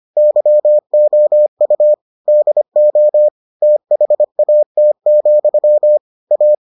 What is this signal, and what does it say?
Morse code, heard as a keyed tone, about 25 wpm, 600 Hz. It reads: YOU DO THAT, A